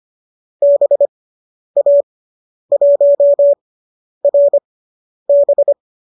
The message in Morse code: BA1RB

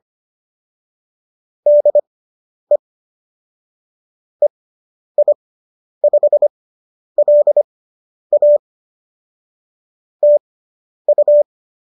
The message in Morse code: DE EI5LA TU

